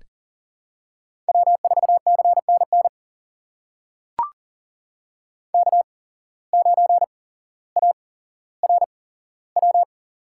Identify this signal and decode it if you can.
Morse code, heard as a keyed tone, about 41 wpm, spaced as for 17 wpm, 700 Hz. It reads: W4CNN K9ARW
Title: USA Call Signs - 5 Characters - 8x Character Spacing - 40wpm